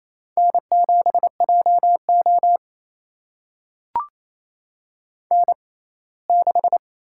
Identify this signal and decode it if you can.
Morse code, heard as a keyed tone, about 28 wpm, spaced as for 19 wpm, 700 Hz. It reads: N7JO N6
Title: USA Call Signs - 4 Characters - 6x Character Spacing - 28wpm